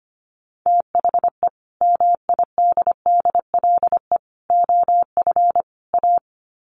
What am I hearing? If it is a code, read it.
Morse code, heard as a keyed tone, about 25 wpm, 700 Hz. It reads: THE MIDDLE OF A
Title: Sets of 4 Words 25wpm